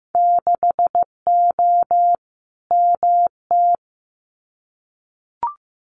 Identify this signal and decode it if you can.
Morse code, heard as a keyed tone, about 15 wpm, 700 Hz. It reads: 6O MT
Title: Winter Field Day Exchanges Encore 15wpm